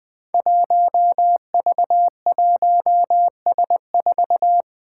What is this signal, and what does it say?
Morse code, heard as a keyed tone, about 21 wpm, 700 Hz. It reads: 1V1S4
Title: Five Challenging Letter-Number Combinations 20wpm